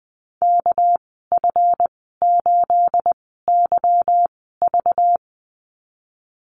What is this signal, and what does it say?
Morse code, heard as a keyed tone, about 20 wpm, spaced as for 15 wpm, 700 Hz. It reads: KF8YV
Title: 219 Lesson 35g - Number 0 - Call Signs 20wpm